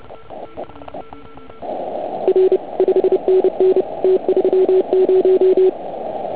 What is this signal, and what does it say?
Morse code, heard as a keyed tone, about 30 wpm, 370 Hz. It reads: R 5NN T30
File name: Archiv / CUC149 / OK1IF CUC149